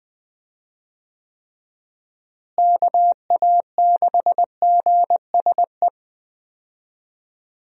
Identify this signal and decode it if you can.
Morse code, heard as a keyed tone, about 20 wpm, 700 Hz. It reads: KA6GSE
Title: USA Call Signs - 6 Characters - Repeated 3x - 20wpm